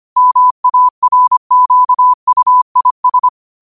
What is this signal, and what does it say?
Morse code, heard as a keyed tone, about 25 wpm, 1000 Hz. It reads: MARQUIS
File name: qotdfiles / 25wpm